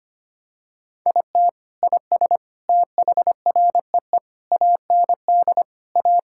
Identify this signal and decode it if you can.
Morse code, heard as a keyed tone, about 25 wpm, 700 Hz. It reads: IT IS THREE AND A
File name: practice / 25wpm / Sentences with 3-Word Phrases - 25wpm